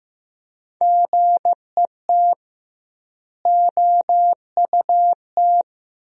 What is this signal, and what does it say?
Morse code, heard as a keyed tone, about 16 wpm, 700 Hz. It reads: GET OUT
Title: 2x Word-Spacing - Sentences from Top 200 Words - 15wpm